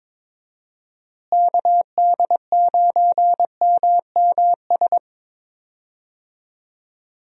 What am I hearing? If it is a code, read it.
Morse code, heard as a keyed tone, about 22 wpm, 700 Hz. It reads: KD9MMS
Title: USA Call Signs - 6 Characters - Repeated 3x - 22wpm